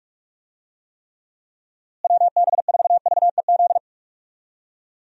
Morse code, heard as a keyed tone, about 45 wpm, 700 Hz. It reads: WB4VEZ
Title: USA Call Signs - 6 Characters - Repeated 2x - 45wpm